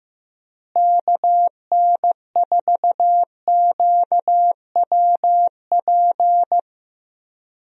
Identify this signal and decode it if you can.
Morse code, heard as a keyed tone, about 15 wpm, 700 Hz. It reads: KN4QWP
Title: USA Call Signs - 6 Characters - Repeated 3x - 15wpm